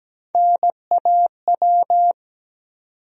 Morse code, heard as a keyed tone, about 17 wpm, 700 Hz. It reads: NAW
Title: USA Call Sign Postfixes - 3 Letters 17wpm